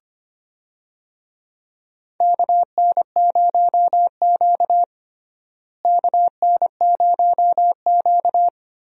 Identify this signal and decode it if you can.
Morse code, heard as a keyed tone, about 25 wpm, 700 Hz. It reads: KN0Q KN0Q
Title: USA Call Signs - 4 Characters - Repeated 3x - 25wpm